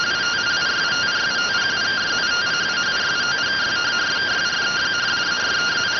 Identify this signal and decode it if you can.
Two-tone teleprinter signal, about 170 Hz shift.